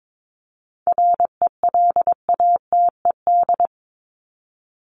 Morse code, heard as a keyed tone, about 22 wpm, 700 Hz. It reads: RELATED
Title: ED Suffix - Top 100 Words with Base Words 22wpm